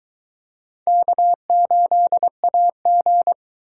Morse code, heard as a keyed tone, about 21 wpm, 700 Hz. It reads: K8AG